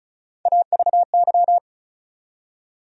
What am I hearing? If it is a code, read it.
Morse code, heard as a keyed tone, about 34 wpm, 700 Hz. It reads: AVY
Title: USA Call Sign Postfixes - 3 Letters 35wpm